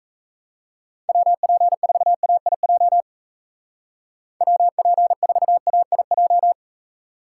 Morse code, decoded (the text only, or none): WP4AIJ WP4AIJ